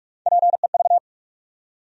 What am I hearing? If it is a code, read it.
Morse code, heard as a keyed tone, about 44 wpm, 700 Hz. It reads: PEV
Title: USA Call Sign Postfixes - 3 Letters 45wpm